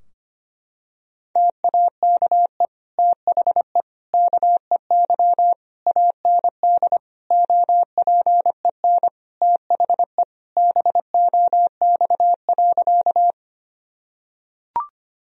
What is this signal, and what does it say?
Morse code, heard as a keyed tone, about 25 wpm, 700 Hz. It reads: TAKE THE KEY AND OPEN THE BOX.
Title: Sentences from Top 1000 Words - Review Part 4 - 25wpm